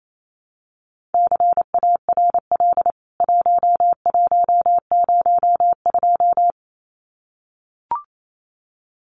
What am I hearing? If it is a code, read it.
Morse code, heard as a keyed tone, about 28 wpm, 700 Hz. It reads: CARL 1102
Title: CWT Exchange 28wpm